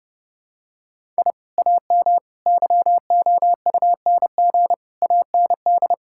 Morse code, heard as a keyed tone, about 32 wpm, 700 Hz. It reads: I AM YOUNG AND
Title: Sentences from Top 1000 Words - Review Part 5 - 30wpm